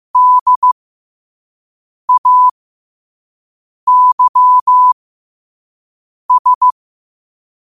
Morse code, heard as a keyed tone, about 15 wpm, spaced as for 5 wpm, 1000 Hz. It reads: DAYS
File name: qotdfiles / 5wpm